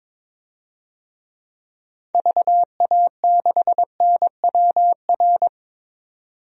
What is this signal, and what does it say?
Morse code, heard as a keyed tone, about 21 wpm, 700 Hz. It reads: VA6NWR
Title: DXCC-100 Call Signs - 6 Characters - 22wpm